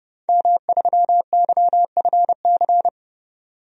Morse code, heard as a keyed tone, about 31 wpm, 700 Hz. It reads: M3YFC
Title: DXCC-100 Call Signs - 5 Characters - 30wpm